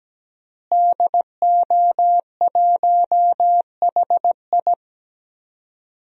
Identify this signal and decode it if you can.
Morse code, heard as a keyed tone, about 17 wpm, 700 Hz. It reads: DO1HI
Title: DXCC-100 Call Signs - 5 Characters - Repeated 2x - 17wpm